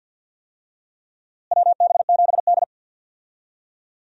Morse code, heard as a keyed tone, about 47 wpm, 700 Hz. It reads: WB7D
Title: USA Call Signs - 4 Characters - Repeated 3x - 50wpm